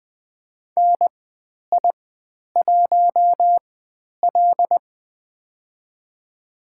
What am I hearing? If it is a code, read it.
Morse code, heard as a keyed tone, about 20 wpm, spaced as for 10 wpm, 700 Hz. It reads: NI1L
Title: 095 Lesson 16g - Letter c - Call Signs 20wpm